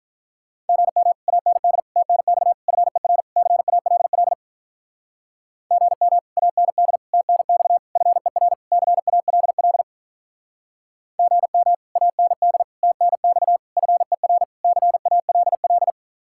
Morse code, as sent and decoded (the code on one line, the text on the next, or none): --. -- / .- -. -.. / - -. -..- / ..-. .-. / -.-. .- .-.. .-.. / --. -- / .- -. -.. / - -. -..- / ..-. .-. / -.-. .- .-.. .-.. / --. -- / .- -. -.. / - -. -..- / ..-. .-. / -.-. .- .-.. .-..
GM AND TNX FER CALL GM AND TNX FER CALL GM AND TNX FER CALL